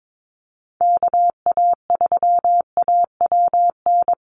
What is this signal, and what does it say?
Morse code, heard as a keyed tone, about 22 wpm, 700 Hz. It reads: KA3AWN
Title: USA Call Signs Repeated 2x 22wpm